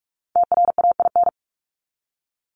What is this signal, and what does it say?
Morse code, heard as a keyed tone, about 44 wpm, 700 Hz. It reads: TRAIN
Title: ING Suffix - Top 100 Words with Base Words 45wpm